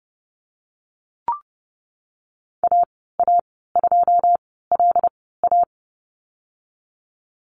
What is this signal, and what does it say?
Morse code, heard as a keyed tone, about 30 wpm, spaced as for 17 wpm, 700 Hz. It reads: AA2LA
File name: Lesson 19g - Letter w - Call Signs 30wpm